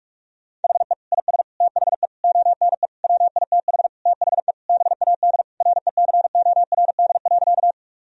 Morse code, heard as a keyed tone, about 47 wpm, 700 Hz. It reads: HE IS THE ONE WITH THE BAD RECORD.
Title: Sentences from Top 1000 Words - Review Part 3 - 45wpm